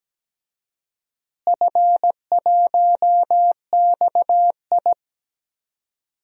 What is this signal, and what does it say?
Morse code, heard as a keyed tone, about 17 wpm, 700 Hz. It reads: F1XI
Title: DXCC-100 Call Signs - 4 Characters - Repeated 3x - 17wpm